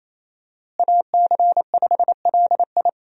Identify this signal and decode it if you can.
Morse code, heard as a keyed tone, about 28 wpm, 700 Hz. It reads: AC5LI
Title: USA Call Signs - 5 Characters - Repeated 2x - 28wpm